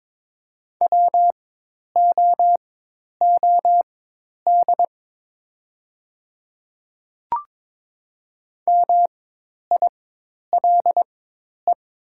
Morse code, heard as a keyed tone, about 22 wpm, spaced as for 10 wpm, 700 Hz. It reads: WOOD MILE
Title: 4x Character Spacing - Top 300 Words - 22wpm